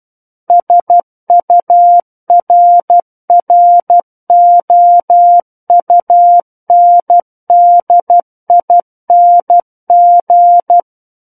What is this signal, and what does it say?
Morse code, heard as a keyed tone, about 12 wpm, 700 Hz. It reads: SURROUNDING
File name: Book One / 03-Chapter 3 - 12 WPM